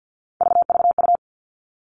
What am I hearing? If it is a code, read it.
Morse code, heard as a keyed tone, about 52 wpm, 700 Hz. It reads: VVU
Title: All Cut Numbers - 3-Digit 50wpm